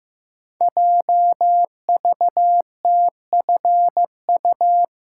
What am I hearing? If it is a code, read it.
Morse code, heard as a keyed tone, about 15 wpm, 700 Hz. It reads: JVTFU
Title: Code Groups - 5 Letter:Numbers - 15wpm